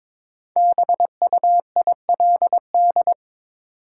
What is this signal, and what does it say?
Morse code, heard as a keyed tone, about 22 wpm, 700 Hz. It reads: BUILD
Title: Sentences from Top 1100 Words - New Word Drill Alt - 22wpm